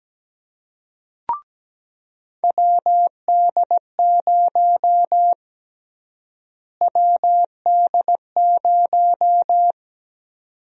Morse code, heard as a keyed tone, about 17 wpm, 700 Hz. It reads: WD0 WD0
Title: USA Call Sign Prefixes - Repeated 2x - 17wpm